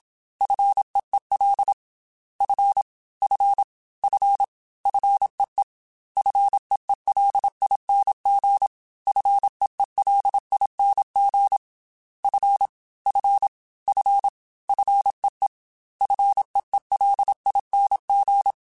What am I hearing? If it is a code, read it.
Morse code, heard as a keyed tone, about 27 wpm, 800 Hz. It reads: FEEL F F F FEE FEELING FEELING F F F FEE FEELING
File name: Single F mp3s / Single F Prefixes 3x2 Twice @25